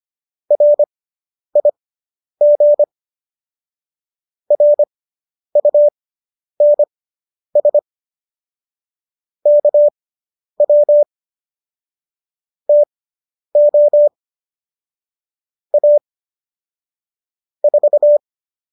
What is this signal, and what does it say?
Morse code, heard as a keyed tone, about 25 wpm, spaced as for 10 wpm, 590 Hz. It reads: RIG RUNS KW TO A 4